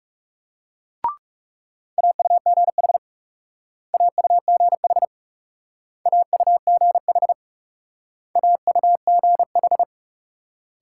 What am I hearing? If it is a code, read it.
Morse code, heard as a keyed tone, about 37 wpm, 700 Hz. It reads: AUGH AUGH AUGH AUGH
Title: Speed-Racing Multi-letter Phonemes - 30wpm